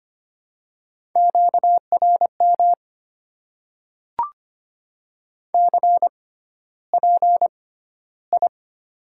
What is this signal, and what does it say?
Morse code, heard as a keyed tone, about 26 wpm, spaced as for 13 wpm, 700 Hz. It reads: QRM CPI
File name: Top 90 QSO Elements - 6x Character Spacing - 25wpm